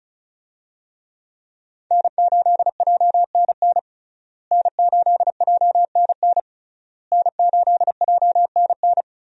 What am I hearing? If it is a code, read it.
Morse code, heard as a keyed tone, about 36 wpm, 700 Hz. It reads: N8JNN N8JNN N8JNN
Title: USA Call Signs - 5 Characters - Repeated 3x - 35wpm